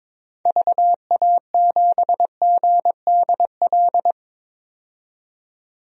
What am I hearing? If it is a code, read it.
Morse code, heard as a keyed tone, about 23 wpm, 700 Hz. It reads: VA7GDL
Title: DXCC-100 Call Signs - 6 Characters - 22wpm